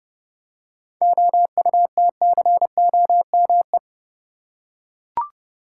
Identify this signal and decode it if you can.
Morse code, heard as a keyed tone, about 30 wpm, 700 Hz. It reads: OUTCOME
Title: Seven Letter Words - 30wpm